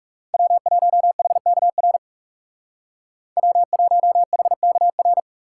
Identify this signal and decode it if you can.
Morse code, heard as a keyed tone, about 36 wpm, 700 Hz. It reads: W1HKR W1HKR
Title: Speed Racing - USA 1x3 Call Signs 30wpm